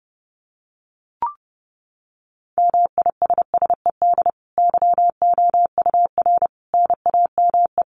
Morse code, heard as a keyed tone, about 30 wpm, 700 Hz. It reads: MISSED YOUR NAME
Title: Common QSO Phrases 30wpm